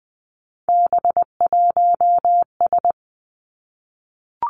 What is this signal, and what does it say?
Morse code, heard as a keyed tone, about 20 wpm, 700 Hz. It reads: B1S
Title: Three Challenging Letter-Number Combinations 20wpm